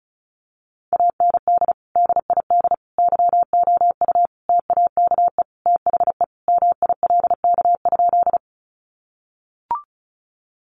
Morse code, heard as a keyed tone, about 34 wpm, 700 Hz. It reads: AND DID YOU TAKE THE MILK?
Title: Sentences from Top 1500 Words 35wpm